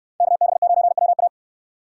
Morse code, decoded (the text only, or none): KB9WA